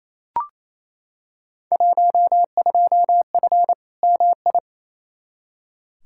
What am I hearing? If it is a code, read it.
Morse code, heard as a keyed tone, about 28 wpm, 700 Hz. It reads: 12F MI
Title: Field Day Exchanges Encore - 28wpm